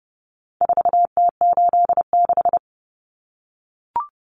Morse code, heard as a keyed tone, about 30 wpm, 700 Hz. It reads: 4T86
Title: Common Cut Numbers - 4-Digit 30wpm